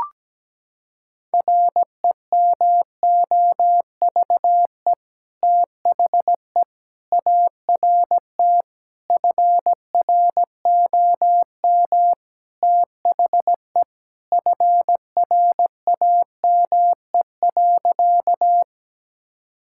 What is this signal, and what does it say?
Morse code, heard as a keyed tone, about 17 wpm, 700 Hz. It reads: REMOVE THE ART FROM THE FRAME.
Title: Sentences from Top 1000 Words - Review Part 6 - 17wpm